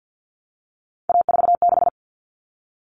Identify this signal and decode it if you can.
Morse code, heard as a keyed tone, about 50 wpm, 700 Hz. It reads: A46
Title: Common Cut Numbers - 3-Digit 50wpm